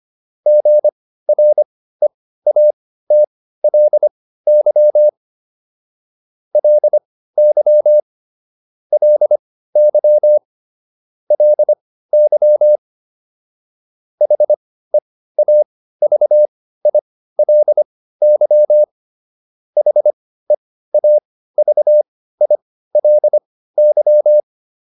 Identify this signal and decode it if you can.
Morse code, heard as a keyed tone, about 25 wpm, spaced as for 15 wpm, 600 Hz. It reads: GREATLY LY LY LY HEAVILY HEAVILY